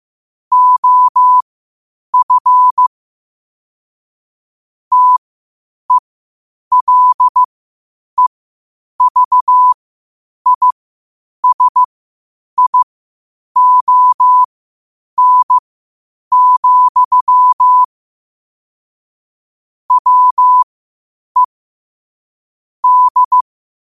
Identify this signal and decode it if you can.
Morse code, heard as a keyed tone, about 15 wpm, spaced as for 8 wpm, 1000 Hz. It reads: OF TELEVISION, WE D
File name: qotdfiles / 8wpm